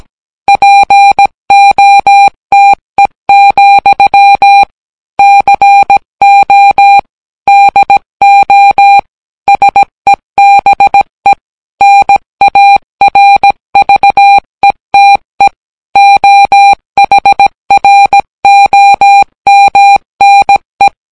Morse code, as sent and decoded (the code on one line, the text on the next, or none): .--. --- - . --..-- / -.-. --- / -.. --- / ... -... / -. .- .-. ...- . - . / --- .... .-. --- -- -. .
POTE, CO DO SEBE NARVETE OHROMNE